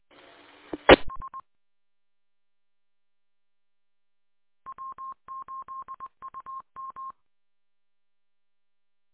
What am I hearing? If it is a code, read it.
Morse code, heard as a keyed tone, about 20 wpm, 1060 Hz. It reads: ES W8UM